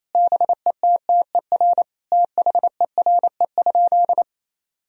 Morse code, heard as a keyed tone, about 28 wpm, 700 Hz. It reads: BETTER THERE?
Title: Sentences from Top 1000 Words - Review Part 4 - 28wpm